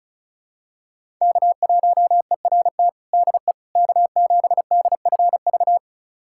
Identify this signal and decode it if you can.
Morse code, heard as a keyed tone, about 34 wpm, 700 Hz. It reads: K1ERT DE K7DFV
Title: USA Callsign DE Callsign - 5 Characters - 35wpm